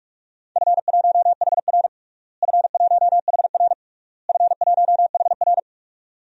Morse code, heard as a keyed tone, about 41 wpm, 700 Hz. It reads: F1HR F1HR F1HR
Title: DXCC-100 Call Signs - 4 Characters - Repeated 3x - 45wpm